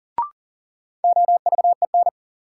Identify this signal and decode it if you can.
Morse code, heard as a keyed tone, about 40 wpm, 700 Hz. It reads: OVEN
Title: Sentences from Top 1200 Words - New Word Drill Alt - 40wpm